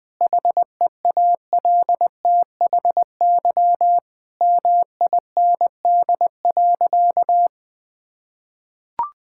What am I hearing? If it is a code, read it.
Morse code, heard as a keyed tone, about 21 wpm, 700 Hz. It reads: HEALTHY MIND.
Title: Sentences from Top 1000 Words - Review Part 5 - 20wpm